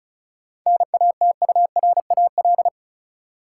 Morse code, heard as a keyed tone, about 36 wpm, 700 Hz. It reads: NATURAL
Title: Sentences from Top 700 Words - New Word Drill - 35wpm